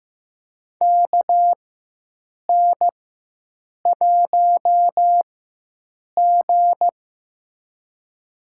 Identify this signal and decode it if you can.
Morse code, heard as a keyed tone, about 15 wpm, spaced as for 7 wpm, 700 Hz. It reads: KN1G
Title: USA Call Signs - 4x Character Spacing - 15wpm